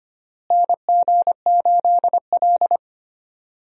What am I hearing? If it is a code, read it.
Morse code, heard as a keyed tone, about 25 wpm, 700 Hz. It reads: NG8L
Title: USA Call Signs - 4 Characters - Repeated 3x - 25wpm